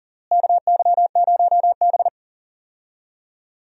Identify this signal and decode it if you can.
Morse code, heard as a keyed tone, about 40 wpm, 700 Hz. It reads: KY0B
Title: USA Call Signs - 4 Characters - Repeated 3x - 40wpm